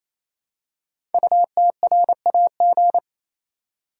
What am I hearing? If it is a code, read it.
Morse code, heard as a keyed tone, about 28 wpm, 700 Hz. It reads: UTRAG
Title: Code Groups - 5 Letter:Numbers - 28wpm